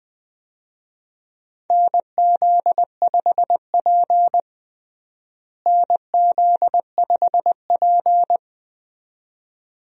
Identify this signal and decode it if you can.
Morse code, heard as a keyed tone, about 20 wpm, 700 Hz. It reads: NZ5P NZ5P
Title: USA Call Signs - 4 Characters - Repeated 2x - 20wpm